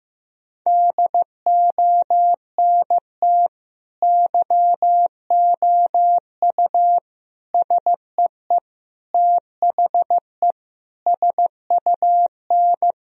Morse code, heard as a keyed tone, about 15 wpm, 700 Hz. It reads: DONT YOU SEE THE SUN
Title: Sentences from Top 1000 Words - Review Part 2 - 15wpm